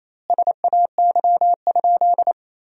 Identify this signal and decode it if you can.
Morse code, heard as a keyed tone, about 27 wpm, 700 Hz. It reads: SAY?
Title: 2x Word-Spacing - Sentences from Top 300 Words - 28wpm